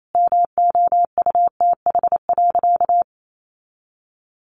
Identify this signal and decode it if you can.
Morse code, heard as a keyed tone, about 28 wpm, 700 Hz. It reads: MOUTH.
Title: Sentences from Top 1500 Words 28wpm